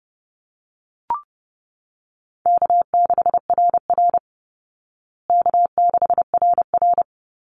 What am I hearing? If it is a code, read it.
Morse code, heard as a keyed tone, about 30 wpm, 700 Hz. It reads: K6RR K6RR
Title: Call Signs Repeated 2x - 30wpm